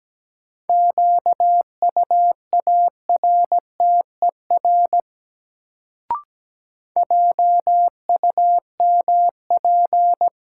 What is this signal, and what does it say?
Morse code, heard as a keyed tone, about 17 wpm, 700 Hz. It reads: QUARTER JUMP
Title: Sentences from Top 1300 Words - New Word Drill Alt - 17wpm